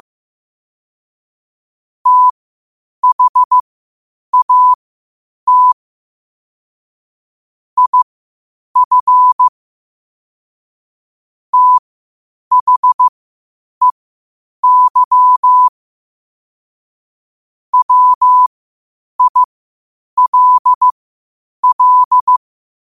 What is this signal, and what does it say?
Morse code, heard as a keyed tone, about 15 wpm, spaced as for 8 wpm, 1000 Hz. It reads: THAT IF THEY WILL
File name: qotdfiles / 8wpm